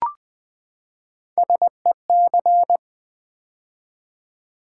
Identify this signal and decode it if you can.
Morse code, heard as a keyed tone, about 20 wpm, 700 Hz. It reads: SEC
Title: Build A Word - 3 Letters At A Time - 6-Letter Words - 20wpm